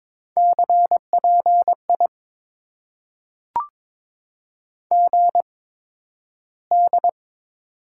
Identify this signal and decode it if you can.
Morse code, heard as a keyed tone, about 22 wpm, spaced as for 12 wpm, 700 Hz. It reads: CPI GD
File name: Top 90 QSO Elements - 8x Character Spacing - 22wpm